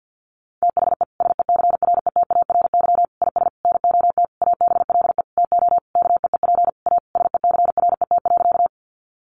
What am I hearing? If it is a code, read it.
Morse code, heard as a keyed tone, about 49 wpm, 700 Hz. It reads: THE SECRETARY IS NOT ABLE TO KEEP A SECRET.
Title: Sentences from Top 1300 Words 50wpm